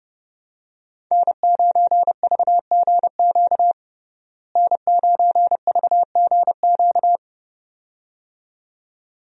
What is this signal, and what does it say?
Morse code, heard as a keyed tone, about 30 wpm, 700 Hz. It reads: N9VGQ N9VGQ
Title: USA Call Signs - 5 Characters - Repeated 2x - 30wpm